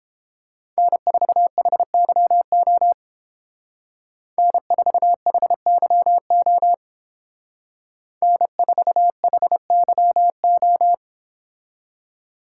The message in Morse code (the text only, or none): N4HYO N4HYO N4HYO